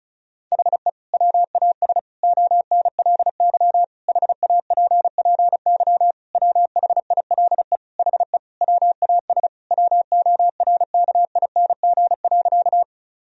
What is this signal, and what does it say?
Morse code, heard as a keyed tone, about 35 wpm, 700 Hz. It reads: HE WAS ONLY HAPPY WHILE HE WAS WORKING.
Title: Sentences from Top 1000 Words - Review Part 4 - 35wpm